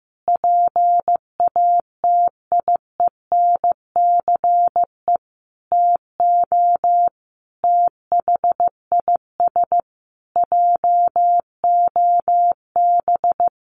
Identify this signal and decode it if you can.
Morse code, heard as a keyed tone, about 15 wpm, 700 Hz. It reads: PATIENCE TO THIS JOB
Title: Sentences from Top 1300 Words 15wpm